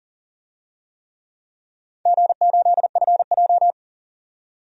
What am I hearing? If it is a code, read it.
Morse code, heard as a keyed tone, about 40 wpm, 700 Hz. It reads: G8FJ
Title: DXCC-100 Call Signs - 4 Characters - 40wpm